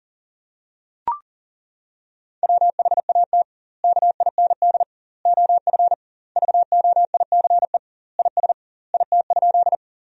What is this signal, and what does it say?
Morse code, heard as a keyed tone, about 37 wpm, 700 Hz. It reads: WHAT KIND OF VOICE IS IT?
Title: 2x Word-Spacing - Sentences from Top 500 Words - 40wpm